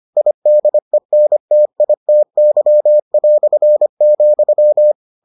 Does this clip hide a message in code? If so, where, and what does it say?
Yes: Morse code from 0.2 to 4.9 s, heard as a keyed tone, about 25 wpm, 600 Hz. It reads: IDENTITY",